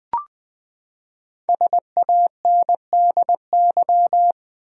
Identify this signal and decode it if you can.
Morse code, heard as a keyed tone, about 20 wpm, 700 Hz. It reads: SANDY
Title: USA Common Names - Max 5 Letters - 20wpm